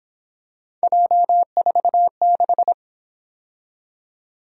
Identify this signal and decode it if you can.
Morse code, heard as a keyed tone, about 26 wpm, 700 Hz. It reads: J46
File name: Speed-Racing - Three Letter-Number Combinations - 22wpm